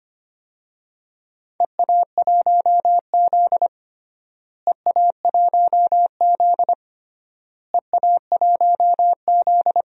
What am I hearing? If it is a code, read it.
Morse code, heard as a keyed tone, about 25 wpm, 700 Hz. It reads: EA1Z EA1Z EA1Z
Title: DXCC-100 Call Signs - 4 Characters - Repeated 3x - 25wpm